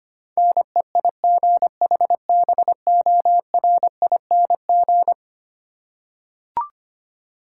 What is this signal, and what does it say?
Morse code, heard as a keyed tone, about 25 wpm, 700 Hz. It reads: NEIGHBORING
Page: morse-code-ninja-practice-25wpm